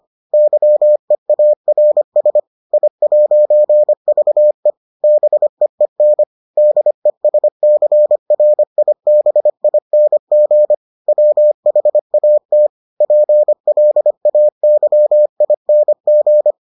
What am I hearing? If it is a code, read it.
Morse code, heard as a keyed tone, about 25 wpm, 600 Hz. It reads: YEARS I'VE BEEN DESCRIBING WHAT PLAYING